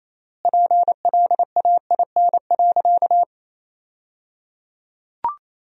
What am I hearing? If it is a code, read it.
Morse code, heard as a keyed tone, about 28 wpm, 700 Hz. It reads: PLAIN.
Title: 3x Word-Spacing - Sentences from Top 400 Words - 28wpm